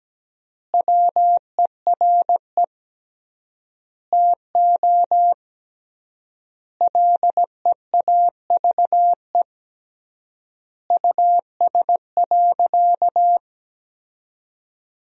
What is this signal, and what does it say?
Morse code, heard as a keyed tone, about 17 wpm, 700 Hz. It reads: WERE TO LEAVE US.
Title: 3x Word-Spacing - Sentences from Top 400 Words - 17wpm